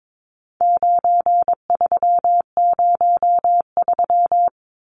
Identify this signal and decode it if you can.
Morse code, heard as a keyed tone, about 22 wpm, 700 Hz. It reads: 9303
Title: Numbers - 4-Digit 22wpm